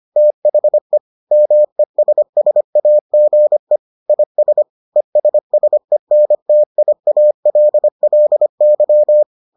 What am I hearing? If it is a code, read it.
Morse code, heard as a keyed tone, about 25 wpm, 600 Hz. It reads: THE MESSAGE IS ESSENTIALLY